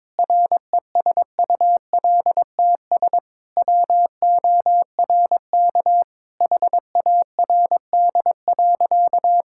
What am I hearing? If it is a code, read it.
Morse code, heard as a keyed tone, about 22 wpm, 700 Hz. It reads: RESULTS WORK HARD.
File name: practice / 22wpm / Sentences with 3-Word Phrases - 22wpm